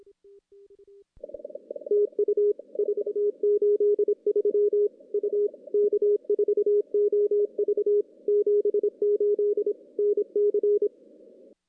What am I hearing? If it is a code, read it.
Morse code, heard as a keyed tone, about 26 wpm, 410 Hz. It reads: TU 483 U K4OV 78 NC
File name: CW / CW-SS-k5tr- -1700